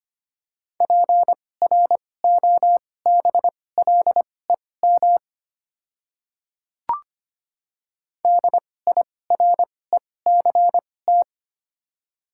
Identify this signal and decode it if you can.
Morse code, heard as a keyed tone, about 25 wpm, spaced as for 18 wpm, 700 Hz. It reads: PROBLEM DIRECT
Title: 2x Character Spacing - Top 500 Words - 25wpm